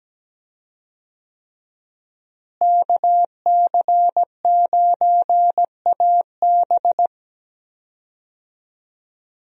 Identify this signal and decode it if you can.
Morse code, heard as a keyed tone, about 17 wpm, 700 Hz. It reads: KC9AB
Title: USA Call Signs - 5 Characters - Repeated 3x - 17wpm